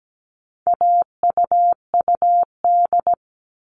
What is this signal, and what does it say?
Morse code, heard as a keyed tone, about 17 wpm, 700 Hz. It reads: AUUD
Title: All Cut Numbers - 4-Digit 17wpm